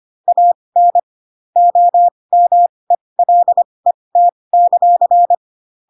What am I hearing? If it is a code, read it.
Morse code, heard as a keyed tone, about 25 wpm, spaced as for 20 wpm, 700 Hz. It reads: AN OMELET;
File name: Pinocchio / Pinocchio ch5 15 25 ik1hge